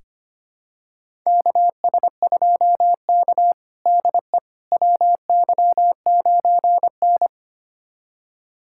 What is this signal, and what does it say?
Morse code, heard as a keyed tone, about 25 wpm, 700 Hz. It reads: KS2K DE WY9N
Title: USA Callsign DE Callsign - 4 Characters - 25wpm